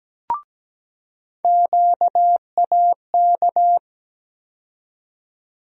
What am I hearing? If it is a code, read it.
Morse code, heard as a keyed tone, about 17 wpm, 700 Hz. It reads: QAK
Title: USA Call Sign Postfixes - 3 Letters 17wpm